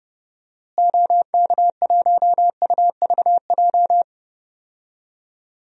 Morse code, heard as a keyed tone, about 30 wpm, 700 Hz. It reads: OK1UVJ
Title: DXCC-100 Call Signs - 6 Characters - Repeated 2x - 30wpm